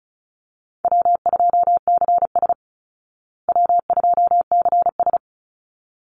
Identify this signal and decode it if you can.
Morse code, heard as a keyed tone, about 34 wpm, 700 Hz. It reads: W2CS W2CS
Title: Easy Call Signs Repeated 2x - 35wpm